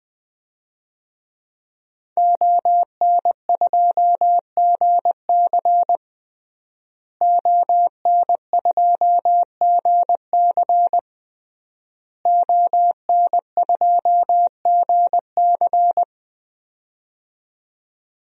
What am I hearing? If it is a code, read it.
Morse code, heard as a keyed tone, about 21 wpm, 700 Hz. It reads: ON2GC ON2GC ON2GC
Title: DXCC-100 Call Signs - 5 Characters - Repeated 3x - 20wpm